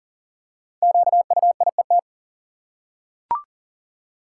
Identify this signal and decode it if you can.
Morse code, heard as a keyed tone, about 40 wpm, 700 Hz. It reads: QUIET
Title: Five Letter Words - 40wpm